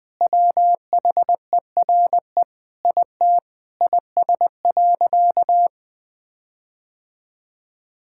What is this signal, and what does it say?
Morse code, heard as a keyed tone, about 20 wpm, 700 Hz. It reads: WHERE IT IS.
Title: Sentences with 4-Word Phrases - 20wpm